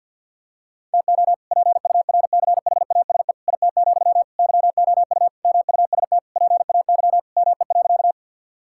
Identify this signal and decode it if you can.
Morse code, heard as a keyed tone, about 49 wpm, 700 Hz. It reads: TO PURCHASE IT, YOU MUST PAY ME.
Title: Sentences from Top 1000 Words - Review Part 5 - 50wpm